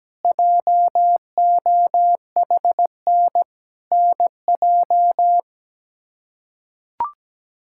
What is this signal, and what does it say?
Morse code, heard as a keyed tone, about 17 wpm, 700 Hz. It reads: JOHN NJ
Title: K1USN SST QSO Practice - 17wpm